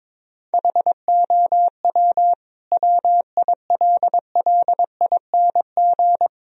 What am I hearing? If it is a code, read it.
Morse code, heard as a keyed tone, about 22 wpm, 700 Hz. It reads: HOW WILLING
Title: Sentences from Top 1000 Words - Review Part 3 - 22wpm